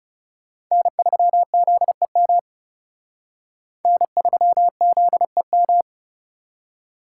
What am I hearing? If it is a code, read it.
Morse code, heard as a keyed tone, about 37 wpm, 700 Hz. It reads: N3ZEM N3ZEM
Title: Speed Racing - USA 1x3 Call Signs 30wpm